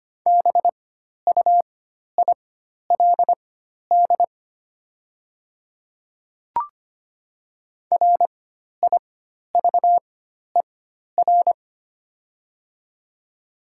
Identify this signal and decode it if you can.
Morse code, heard as a keyed tone, about 25 wpm, spaced as for 12 wpm, 700 Hz. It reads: BUILD RIVER
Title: 4x Character Spacing - Top 300 Words - 25wpm